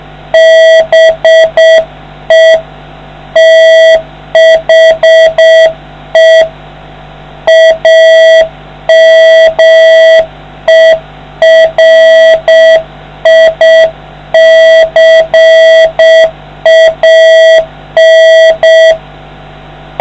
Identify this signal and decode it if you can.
Morse code, heard as a keyed tone, about 7 wpm, 650 Hz, with noise. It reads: BE THE AMERICAN